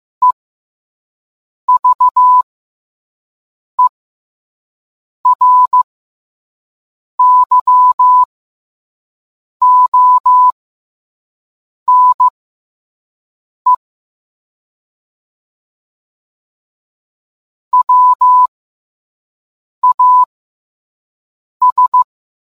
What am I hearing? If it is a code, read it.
Morse code, heard as a keyed tone, about 15 wpm, spaced as for 5 wpm, 1000 Hz. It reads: EVERYONE WAS